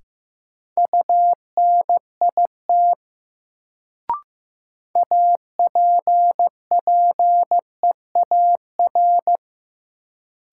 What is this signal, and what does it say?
Morse code, heard as a keyed tone, about 15 wpm, 700 Hz. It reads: UNIT APPEAR
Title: Sentences from Top 1000 Words - Review Part 2 - New Word Drill - 15wpm